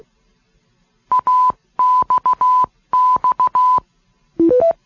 Morse code, heard as a keyed tone, about 16 wpm, 1000 Hz. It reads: AXX